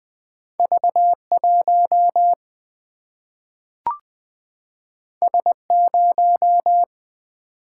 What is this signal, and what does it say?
Morse code, heard as a keyed tone, about 20 wpm, 700 Hz. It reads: V 1 S 0
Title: Single-Letter Single-Number 20wpm